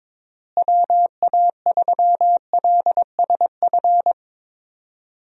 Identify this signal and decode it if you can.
Morse code, heard as a keyed tone, about 22 wpm, 700 Hz. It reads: WA3LSF
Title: USA Call Signs - 6 Characters - Repeated 2x - 22wpm